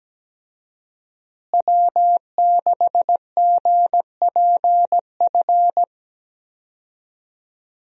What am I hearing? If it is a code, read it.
Morse code, heard as a keyed tone, about 17 wpm, 700 Hz. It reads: W6GPF
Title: USA Call Signs - 5 Characters - Repeated 3x - 17wpm